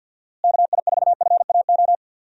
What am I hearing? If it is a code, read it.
Morse code, heard as a keyed tone, about 53 wpm, 700 Hz. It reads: KI4FAO
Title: USA Call Signs - 6 Characters - Repeated 3x - 50wpm